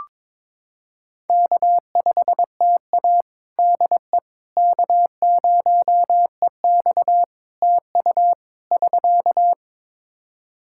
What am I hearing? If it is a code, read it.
Morse code, heard as a keyed tone, about 22 wpm, 700 Hz. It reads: K5TA DE K0EX TU <SK>
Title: Practice QSO - Exchange 4 - Ending - 22wpm